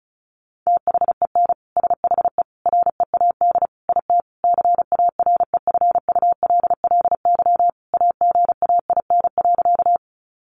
Morse code, heard as a keyed tone, about 34 wpm, 700 Hz. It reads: THEN SHE READ IT CAREFULLY AGAIN.
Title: Sentences from Top 1300 Words 35wpm